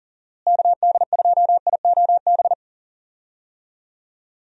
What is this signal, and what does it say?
Morse code, heard as a keyed tone, about 40 wpm, 700 Hz. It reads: KD2IOB
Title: USA Call Signs - 6 Characters - Repeated 3x - 40wpm